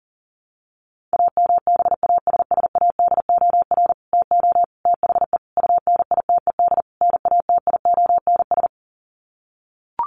Morse code, heard as a keyed tone, about 40 wpm, 700 Hz. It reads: AMBASSADOR TO THE UNITED NATIONS
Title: Sets of 5 Words Encore2 40wpm